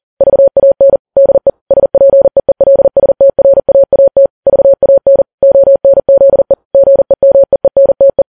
Morse code, heard as a keyed tone, about 40 wpm, 550 Hz. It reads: VAN DE SPEELSTRAAT VAN ONZE GEMEENTE
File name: Files Web tls / Morse on9fv / morse40kps